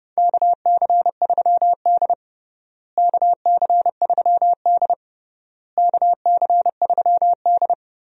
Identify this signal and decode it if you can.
Morse code, heard as a keyed tone, about 30 wpm, 700 Hz. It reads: KC3D KC3D KC3D
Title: USA Call Signs - 4 Characters - Repeated 3x - 30wpm